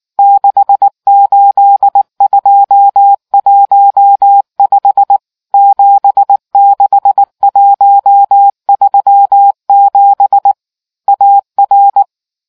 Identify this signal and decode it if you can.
Morse code, heard as a keyed tone, about 19 wpm, 800 Hz. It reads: 68215 76137 AR